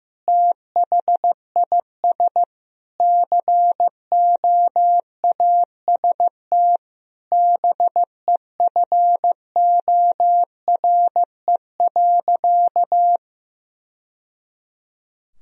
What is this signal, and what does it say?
Morse code, heard as a keyed tone, about 15 wpm, 700 Hz. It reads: THIS COAST BEFORE.
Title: Sentences from Top 1000 Words - Review Part 4 - 15wpm